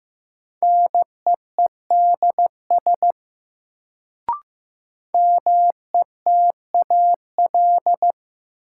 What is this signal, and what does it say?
Morse code, heard as a keyed tone, about 15 wpm, 700 Hz. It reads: NEEDS METAL
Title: Sentences from Top 1600 Words - New Word Drill Alt - 15wpm